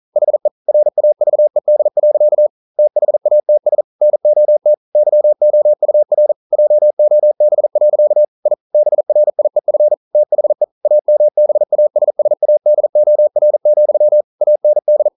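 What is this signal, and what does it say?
Morse code, heard as a keyed tone, about 40 wpm, 600 Hz. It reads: HE RAVED. THATS NOT YOUR JOB. I BRIEF THE AMBASSADOR, AND
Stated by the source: Morse code practice at 41 wpm, from CW med Gnister.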